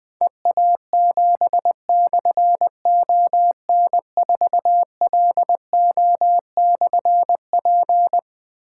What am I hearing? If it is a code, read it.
Morse code, heard as a keyed tone, about 21 wpm, 700 Hz. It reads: EA7/ON4LO/P
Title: Difficult Call Signs 20wpm